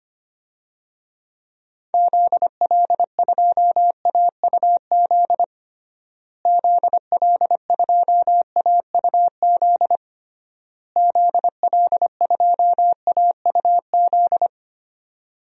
Morse code, heard as a keyed tone, about 25 wpm, 700 Hz. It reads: ZL2AUZ ZL2AUZ ZL2AUZ
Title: DXCC-100 Call Signs - 6 Characters - Repeated 3x - 25wpm